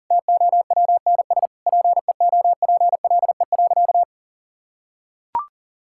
Morse code, heard as a keyed tone, about 40 wpm, 700 Hz. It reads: TOWNS PEOPLE.
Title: Sentences from Top 1000 Words - Review Part 6 - 40wpm